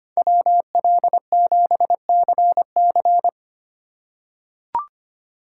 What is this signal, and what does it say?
Morse code, heard as a keyed tone, about 25 wpm, 700 Hz. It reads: WL7CC
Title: USA Call Signs - 5 Characters - 4x Character Spacing - 25wpm